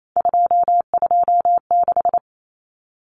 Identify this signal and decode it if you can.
Morse code, heard as a keyed tone, about 28 wpm, 700 Hz. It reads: 226
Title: Common Cut Numbers - 3-Digit 28wpm